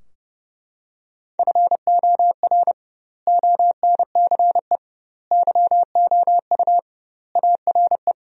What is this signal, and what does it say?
Morse code, heard as a keyed tone, about 32 wpm, 700 Hz. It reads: FOR ONCE YOU ARE
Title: 2x Word-Spacing - Sentences from Top 300 Words - 30wpm